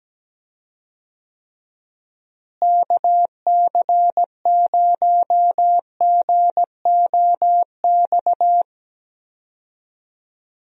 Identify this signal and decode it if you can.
Morse code, heard as a keyed tone, about 17 wpm, 700 Hz. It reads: KC0GOX
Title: USA Call Signs - 6 Characters - Repeated 3x - 17wpm